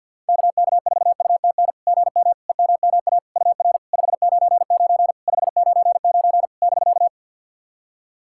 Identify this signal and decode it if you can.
Morse code, heard as a keyed tone, about 48 wpm, 690 Hz. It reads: KK4UTN GM EMMA UR 599 599 <BK>
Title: SOTA QSO Practice 50wpm